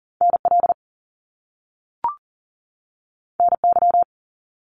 Morse code, heard as a keyed tone, about 40 wpm, 700 Hz. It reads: NL NY
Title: Top Two Letter Combinations in Words 40wpm